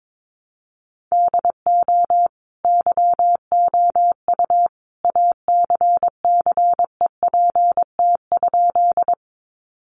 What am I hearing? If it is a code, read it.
Morse code, heard as a keyed tone, about 22 wpm, 700 Hz. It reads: DO YOU ACCEPT?
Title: Sentences from Top 1300 Words 22wpm